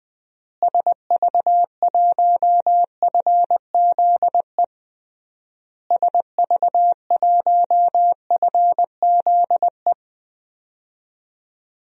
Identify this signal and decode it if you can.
Morse code, heard as a keyed tone, about 20 wpm, 700 Hz. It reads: SV1FZE SV1FZE
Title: DXCC-100 Call Signs - 6 Characters - Repeated 3x - 20wpm